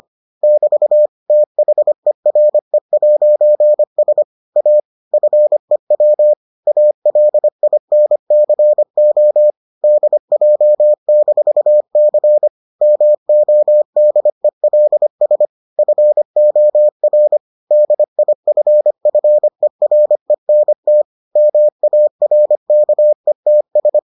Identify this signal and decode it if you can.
Morse code, heard as a keyed tone, about 25 wpm, 600 Hz. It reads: <BT> THERE'S A FEW ALINCO DJ-C MODELS FOR DIFFERENT MARKETS